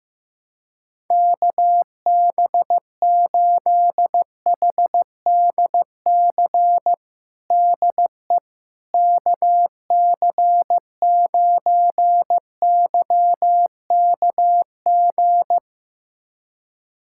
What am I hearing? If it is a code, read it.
Morse code, heard as a keyed tone, about 16 wpm, 700 Hz. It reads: KB8HDC DE KC9YKG
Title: USA Callsign DE Callsign - 6 Characters - 15wpm